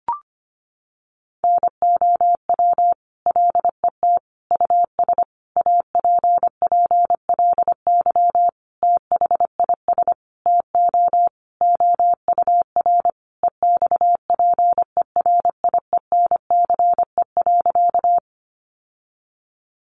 Morse code, heard as a keyed tone, about 25 wpm, 700 Hz. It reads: NOW LET US APPLY THIS TO OUR EXPERIENCE.
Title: Sentences from Top 1200 Words 25wpm